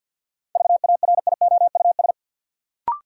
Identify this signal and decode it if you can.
Morse code, heard as a keyed tone, about 51 wpm, 700 Hz. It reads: VARIOUS
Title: Sentences from Top 1000 Words - Review Part 4 - New Word Drill - 50wpm